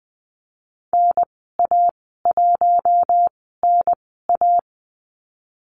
Morse code, heard as a keyed tone, about 19 wpm, spaced as for 14 wpm, 700 Hz. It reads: NA1NA
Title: 048 Lesson 9g - Number 4 - Call Signs 20wpm